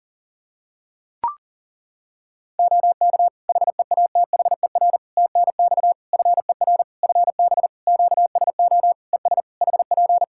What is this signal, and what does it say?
Morse code, heard as a keyed tone, about 40 wpm, 700 Hz. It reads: OK HEATHER TNX FER FB QSO ES HP
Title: K1USN SST QSO Practice 40wpm